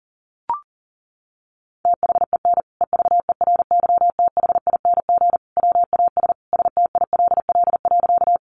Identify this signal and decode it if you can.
Morse code, heard as a keyed tone, about 40 wpm, 700 Hz. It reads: THEN EVERYTHING WAS STILL.
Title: Sentences from Top 700 Words 40wpm